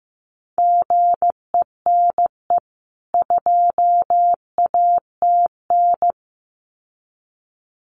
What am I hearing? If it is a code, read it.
Morse code, heard as a keyed tone, about 15 wpm, 700 Hz. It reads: GENE 2ATN
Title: CWT Exchange 15wpm